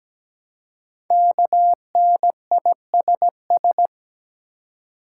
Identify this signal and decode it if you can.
Morse code, heard as a keyed tone, about 17 wpm, 700 Hz. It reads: KNISS
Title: Code Groups - 5 Letters - 17wpm